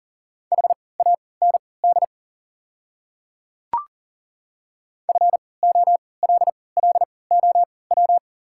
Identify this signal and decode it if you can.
Morse code, heard as a keyed tone, about 40 wpm, spaced as for 23 wpm, 690 Hz. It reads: HAND FOLLOW
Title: 3x Character Spacing - Top 200 Words - 40wpm